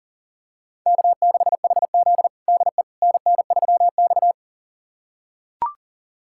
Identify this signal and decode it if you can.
Morse code, heard as a keyed tone, about 40 wpm, 700 Hz. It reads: K6HZ DE NN3X
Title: USA Callsign DE Callsign - 4 Characters - 40wpm